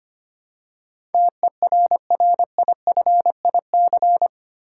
Morse code, heard as a keyed tone, about 25 wpm, 700 Hz. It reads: TERRIFIC
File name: 8 Letter Words - First 3 Letters Given - 25wpm